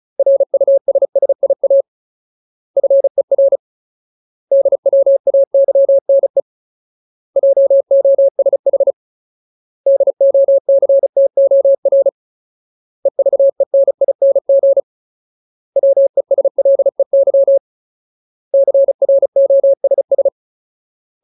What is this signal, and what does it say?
Morse code, heard as a keyed tone, about 34 wpm, 550 Hz. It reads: RUSSIA FER DWAYNE JOSH DOCTOR EVENING WESLEY CROSS
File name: Ultimate Word List / 35WPM Ultimate